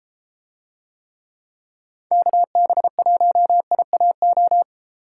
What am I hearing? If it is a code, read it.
Morse code, heard as a keyed tone, about 31 wpm, 700 Hz. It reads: KB1IAO